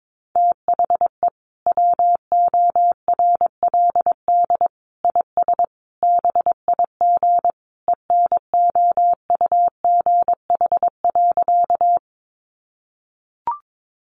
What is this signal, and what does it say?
Morse code, heard as a keyed tone, about 22 wpm, 700 Hz. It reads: THE WORLD IS BIG ENOUGH.
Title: Sentences from Top 400 Words 22wpm